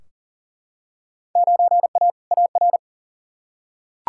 Morse code, heard as a keyed tone, about 40 wpm, 700 Hz. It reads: 9A AR
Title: Field Day Exchanges Encore - 40wpm